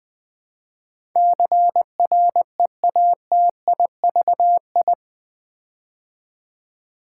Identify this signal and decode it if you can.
Morse code, heard as a keyed tone, about 20 wpm, 700 Hz. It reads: CREATIVI
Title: Build A Word - 4 Letters At A Time - 10-Letter Words - 20wpm